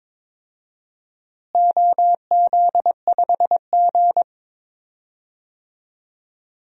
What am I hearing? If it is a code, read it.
Morse code, heard as a keyed tone, about 22 wpm, 700 Hz. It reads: OZ5G
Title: DXCC-100 Call Signs - 4 Characters - Repeated 3x - 22wpm